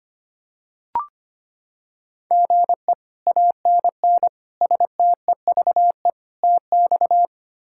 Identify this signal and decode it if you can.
Morse code, heard as a keyed tone, about 25 wpm, 700 Hz. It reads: GE ANN STEVE TX
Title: K1USN SST QSO Practice - 25wpm